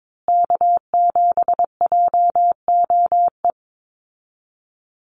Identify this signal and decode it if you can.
Morse code, heard as a keyed tone, about 22 wpm, 700 Hz. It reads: K7JOE
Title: Call Signs Repeated 2x - 22wpm